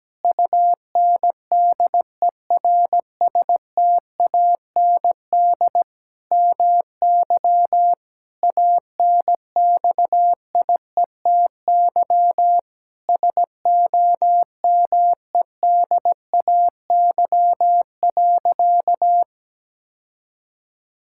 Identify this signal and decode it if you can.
Morse code, heard as a keyed tone, about 17 wpm, 700 Hz. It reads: UNDERSTAND MY ANXIETY SOMEDAY.
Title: Sentences from Top 1600 Words 17wpm